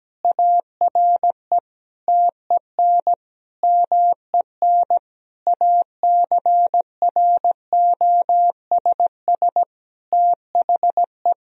Morse code, heard as a keyed tone, about 17 wpm, 700 Hz. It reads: ARE TEN MEN ACROSS THE
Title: Sentences from Top 1000 Words - Review Part 2 - 17wpm